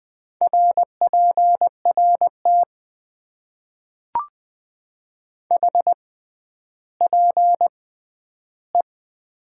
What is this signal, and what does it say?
Morse code, heard as a keyed tone, about 20 wpm, spaced as for 11 wpm, 700 Hz. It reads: RPRT HPE
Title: Top 90 QSO Elements - 6x Character Spacing - 20wpm